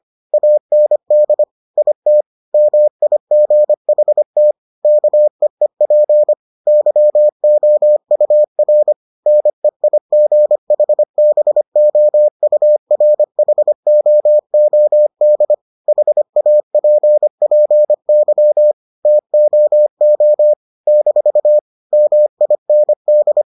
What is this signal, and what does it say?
Morse code, heard as a keyed tone, about 25 wpm, 600 Hz. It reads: AND IT MIGHT KEEP YOUR NEIGHBOURHOOD HAPPY TOO - MIND